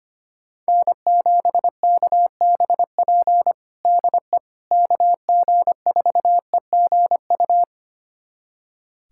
Morse code, heard as a keyed tone, about 25 wpm, 700 Hz. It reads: N7KBP DE KG4EGU
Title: USA Callsign DE Callsign 25wpm